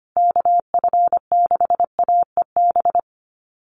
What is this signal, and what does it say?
Morse code, heard as a keyed tone, about 25 wpm, 700 Hz. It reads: KF6AEB
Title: USA Call Signs Repeated 3x 25wpm